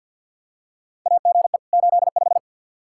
Morse code, heard as a keyed tone, about 47 wpm, 700 Hz. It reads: AGE 85
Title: Practice QSO - Exchange 3 - More Info - 50wpm